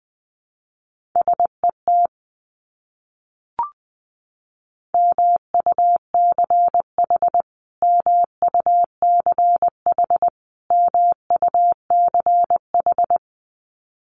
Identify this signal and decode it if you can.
Morse code, heard as a keyed tone, about 20 wpm, 700 Hz. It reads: SET MUCH MUCH MUCH
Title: Top 200 Words Repeated 20wpm